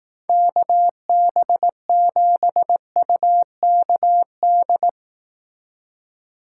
Morse code, heard as a keyed tone, about 18 wpm, 700 Hz. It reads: KB7UKD
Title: Speed Racing - USA 2x3 Call Signs 15wpm